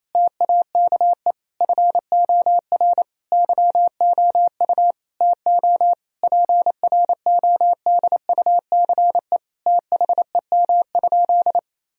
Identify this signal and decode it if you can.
Morse code, heard as a keyed tone, about 28 wpm, 700 Hz. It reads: TAKE FOR YOU TO PRODUCE THEM?
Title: Sentences from Top 1000 Words - Review Part 3 - 28wpm